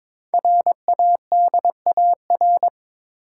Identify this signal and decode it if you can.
Morse code, heard as a keyed tone, about 22 wpm, 700 Hz. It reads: RADAR
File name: 5 Letter Words - First 3 Letters Given - 22wpm